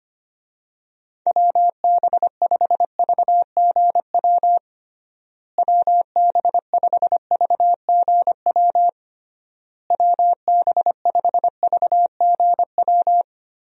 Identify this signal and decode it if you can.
Morse code, heard as a keyed tone, about 25 wpm, 700 Hz. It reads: WB5VGW WB5VGW WB5VGW
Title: USA Call Signs - 6 Characters - Repeated 3x - 25wpm